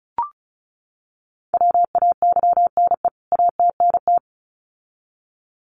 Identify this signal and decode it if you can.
Morse code, heard as a keyed tone, about 34 wpm, 700 Hz. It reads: WAYNE ATNT
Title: CWT Exchange 35wpm